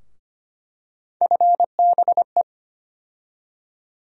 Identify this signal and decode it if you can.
Morse code, heard as a keyed tone, about 25 wpm, 700 Hz. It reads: FBE
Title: USA Call Sign Postfixes - 3 Letters 25wpm